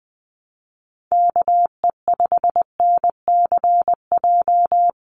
Morse code, heard as a keyed tone, about 20 wpm, 700 Hz. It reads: KE5NCJ
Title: USA Call Signs Repeated 3x 20wpm